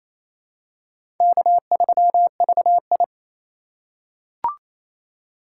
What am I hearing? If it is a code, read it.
Morse code, heard as a keyed tone, about 28 wpm, 700 Hz. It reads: K3VI
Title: USA Call Signs - 4 Characters - 2x Character Spacing - 28wpm